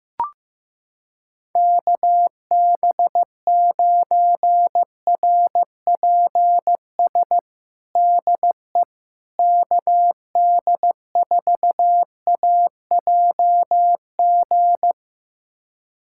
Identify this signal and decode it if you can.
Morse code, heard as a keyed tone, about 15 wpm, 700 Hz. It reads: KB9RPS DE KD4AJG
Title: USA Callsign DE Callsign - 6 Characters - 15wpm